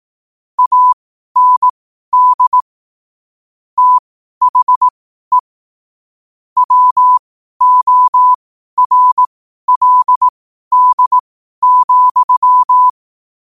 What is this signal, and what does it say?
Morse code, heard as a keyed tone, about 18 wpm, spaced as for 12 wpm, 1000 Hz. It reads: AND THE WORLD,
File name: qotdfiles / 12wpm